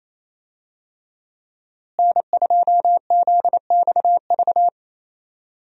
Morse code, heard as a keyed tone, about 27 wpm, 700 Hz. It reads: N2ZXV
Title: USA Call Signs - 5 Characters - Repeated 3x - 28wpm